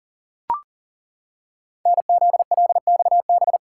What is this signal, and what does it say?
Morse code, heard as a keyed tone, about 40 wpm, 700 Hz. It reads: NZLXB
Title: Code Groups - 5 Letters - 40wpm